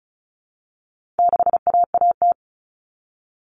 Morse code, heard as a keyed tone, about 34 wpm, 700 Hz. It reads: 6AAT
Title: Common Cut Numbers - 4-Digit 35wpm